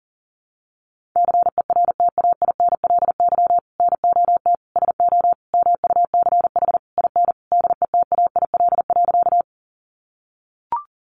Morse code, heard as a keyed tone, about 40 wpm, 700 Hz. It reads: CERTAINLY NOT SO MUCH IN DETAIL.
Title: Sentences from Top 1500 Words 40wpm